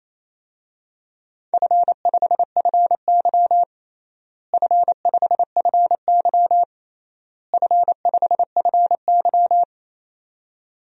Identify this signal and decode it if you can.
Morse code, heard as a keyed tone, about 28 wpm, 700 Hz. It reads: F5FY F5FY F5FY
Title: DXCC-100 Call Signs - 4 Characters - Repeated 3x - 28wpm